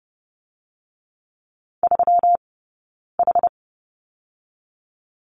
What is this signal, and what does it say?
Morse code, heard as a keyed tone, about 30 wpm, 700 Hz. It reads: 3 H
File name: Lesson 20d - Number 3 - Two Characters 30wpm